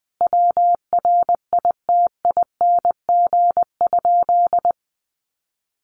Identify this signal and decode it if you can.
Morse code, heard as a keyed tone, about 20 wpm, 700 Hz. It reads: WRITING?
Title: Sentences from Top 1200 Words 20wpm